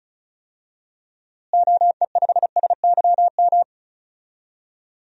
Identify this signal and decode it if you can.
Morse code, heard as a keyed tone, about 36 wpm, 700 Hz. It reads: OE5SYM
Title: DXCC-100 Call Signs - 6 Characters - Repeated 2x - 35wpm